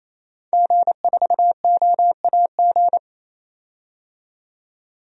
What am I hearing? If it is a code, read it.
Morse code, heard as a keyed tone, about 28 wpm, 700 Hz. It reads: G4OAG
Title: DXCC-100 Call Signs - 5 Characters - 28wpm